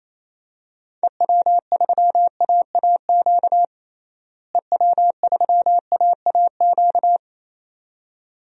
Morse code, heard as a keyed tone, about 27 wpm, 700 Hz. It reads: EW3AAQ EW3AAQ
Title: DXCC-100 Call Signs - 6 Characters - Repeated 2x - 28wpm